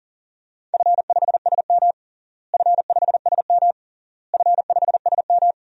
Morse code, heard as a keyed tone, about 37 wpm, 700 Hz. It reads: F5SM F5SM F5SM
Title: DXCC-100 Call Signs - 4 Characters - Repeated 3x - 40wpm